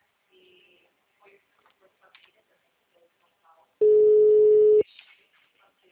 Ringback tone (Europe, 425 Hz).